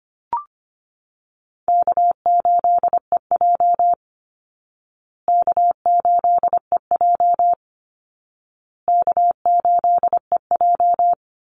Morse code, heard as a keyed tone, about 25 wpm, 700 Hz. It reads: K8EJ K8EJ K8EJ
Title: Call Signs Repeated 2x - 25wpm